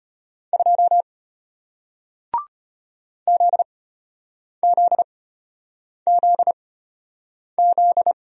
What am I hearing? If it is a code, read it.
Morse code, heard as a keyed tone, about 33 wpm, spaced as for 8 wpm, 700 Hz. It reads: 2 ZZZZ